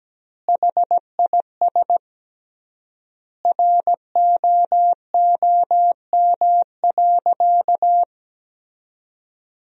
Morse code, heard as a keyed tone, about 17 wpm, 700 Hz. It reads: HIS ROOM.
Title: 3x Word-Spacing - Sentences from Top 300 Words - 17wpm